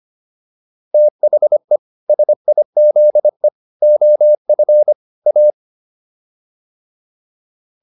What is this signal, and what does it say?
Morse code, heard as a keyed tone, about 25 wpm, 600 Hz. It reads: THE SIZE OF A